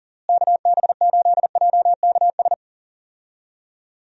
Morse code, heard as a keyed tone, about 40 wpm, 700 Hz. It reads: KB8JKS
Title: USA Call Signs - 6 Characters - Repeated 2x - 40wpm